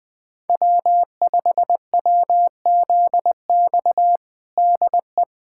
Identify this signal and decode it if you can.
Morse code, heard as a keyed tone, about 21 wpm, 700 Hz. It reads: W5WZX DE
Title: USA Callsign DE Callsign 20wpm